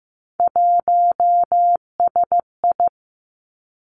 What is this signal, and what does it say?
Morse code, heard as a keyed tone, about 15 wpm, 700 Hz. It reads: 1SI
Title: Three Challenging Letter-Number Combinations 15wpm